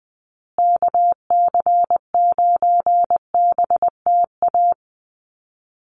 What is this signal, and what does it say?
Morse code, heard as a keyed tone, about 20 wpm, 700 Hz. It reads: KC9BTA
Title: USA Call Signs Repeated 2x 20wpm